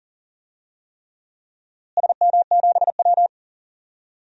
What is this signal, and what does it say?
Morse code, heard as a keyed tone, about 39 wpm, 700 Hz. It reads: SM7W
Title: DXCC-100 Call Signs - 4 Characters - Repeated 3x - 40wpm